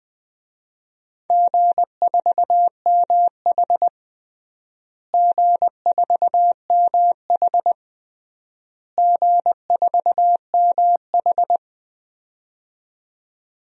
Morse code, heard as a keyed tone, about 21 wpm, 700 Hz. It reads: G4MH G4MH G4MH
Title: DXCC-100 Call Signs - 4 Characters - Repeated 3x - 20wpm